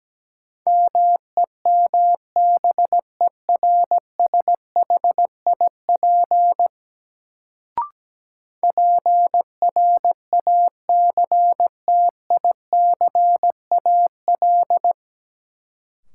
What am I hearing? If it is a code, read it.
Morse code, heard as a keyed tone, about 17 wpm, 700 Hz. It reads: MEMBERSHIP PRACTICAL
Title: Sentences from Top 1400 Words - New Word Drill - 17wpm